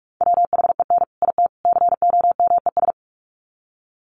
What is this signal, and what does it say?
Morse code, heard as a keyed tone, about 44 wpm, 700 Hz. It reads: WHEN IT COMES
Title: Sets of 3 Words 45wpm